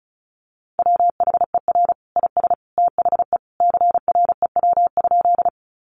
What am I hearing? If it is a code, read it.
Morse code, heard as a keyed tone, about 34 wpm, 700 Hz. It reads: WHER IS THE CREW?
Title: Sentences from Top 1500 Words 35wpm